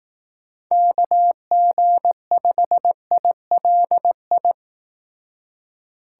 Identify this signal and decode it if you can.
Morse code, heard as a keyed tone, about 20 wpm, 700 Hz. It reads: KG5ILI